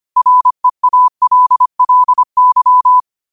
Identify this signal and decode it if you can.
Morse code, heard as a keyed tone, about 25 wpm, 1000 Hz. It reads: REALLY
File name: qotdfiles / 25wpm